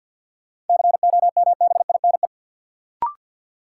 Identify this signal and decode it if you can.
Morse code, heard as a keyed tone, about 53 wpm, 700 Hz. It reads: COMBINE
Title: Sentences from Top 1600 Words - New Word Drill Alt - 50wpm